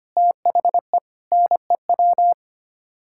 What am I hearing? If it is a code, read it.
Morse code, heard as a keyed tone, about 25 wpm, 700 Hz. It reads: THE NEW
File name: Sets of 9 Words - 25wpm